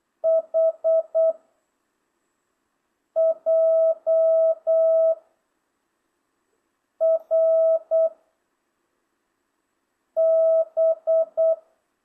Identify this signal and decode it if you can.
Morse code, heard as a keyed tone, about 8 wpm, 660 Hz. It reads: H J R B